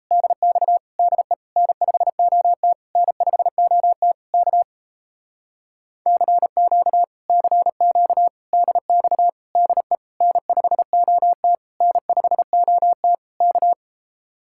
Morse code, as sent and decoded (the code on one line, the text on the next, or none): -.. -..- / -.. / -. ..... --- - / -. ..... --- - / -.- / -.-. --.- / -.-. --.- / -.. -..- / -.. / -. ..... --- - / -. ..... --- - / -.-
DX DE N5OT N5OT K CQ CQ DX DE N5OT N5OT K